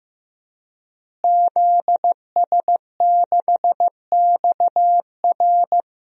Morse code, heard as a keyed tone, about 15 wpm, 700 Hz. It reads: ZS6XR
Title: DXCC-100 Call Signs - 5 Characters - Repeated 3x - 15wpm